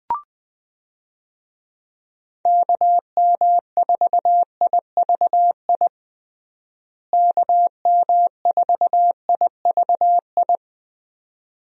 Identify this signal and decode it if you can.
Morse code, heard as a keyed tone, about 20 wpm, 700 Hz. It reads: KM4IVI KM4IVI
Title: USA Call Signs - 6 Characters - Repeated 2x - 20wpm